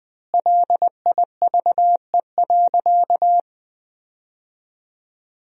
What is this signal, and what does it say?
Morse code, heard as a keyed tone, about 20 wpm, 700 Hz. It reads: LIVE.
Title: 3x Word-Spacing - Sentences from Top 200 Words - 20wpm